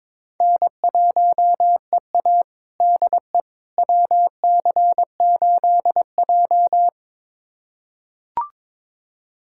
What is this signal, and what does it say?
Morse code, heard as a keyed tone, about 22 wpm, 700 Hz. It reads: N1EA DE WC8J
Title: USA Callsign DE Callsign - 4 Characters - 22wpm